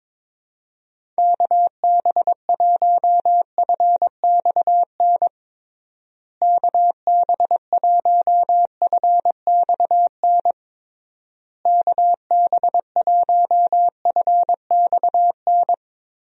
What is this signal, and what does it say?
Morse code, heard as a keyed tone, about 22 wpm, 700 Hz. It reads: KB1FXN KB1FXN KB1FXN
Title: USA Call Signs - 6 Characters - Repeated 3x - 22wpm